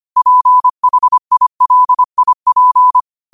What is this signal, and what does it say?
Morse code, heard as a keyed tone, about 25 wpm, 1000 Hz. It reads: PHILIP
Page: Quotes for Wed, 13 Aug 2025 in Morse Code at 25 words per minute.